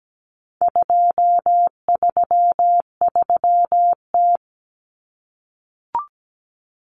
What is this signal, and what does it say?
Morse code, heard as a keyed tone, about 17 wpm, 700 Hz. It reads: 233T
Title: Common Cut Numbers - 4-Digit 17wpm